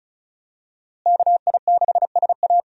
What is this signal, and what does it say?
Morse code, heard as a keyed tone, about 36 wpm, 700 Hz. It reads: KI6SA
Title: USA Call Signs - 5 Characters - Repeated 2x - 35wpm